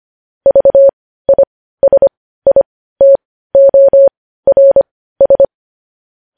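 Morse code, heard as a keyed tone, about 25 wpm, spaced as for 15 wpm, 550 Hz. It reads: VISITORS
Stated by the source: Morse code news headlines at 15.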